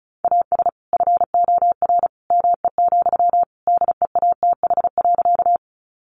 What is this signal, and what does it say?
Morse code, heard as a keyed tone, about 34 wpm, 700 Hz. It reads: AS FOR ME, DEATH.
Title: Sentences from Top 1100 Words 35wpm